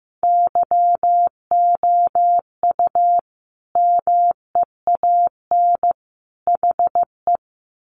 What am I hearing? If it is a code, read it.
Morse code, heard as a keyed tone, about 15 wpm, 700 Hz. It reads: YOU MEAN HE
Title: Sentences from Top 1200 Words 15wpm